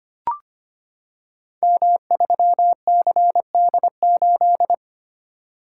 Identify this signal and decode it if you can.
Morse code, heard as a keyed tone, about 25 wpm, 700 Hz. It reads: M3CD8
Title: Code Groups - 5 Letter:Numbers - 25wpm